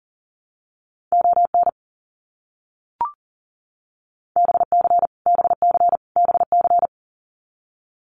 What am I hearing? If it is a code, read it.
Morse code, heard as a keyed tone, about 40 wpm, 700 Hz. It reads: ON BC BC BC
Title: Canadian Provinces and Territories - Repeated 3x - 40wpm